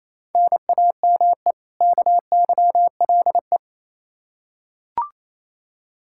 Morse code, heard as a keyed tone, about 28 wpm, 700 Hz. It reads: NAME KYLE
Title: Practice QSO - Exchange 1 - Beginning - 28wpm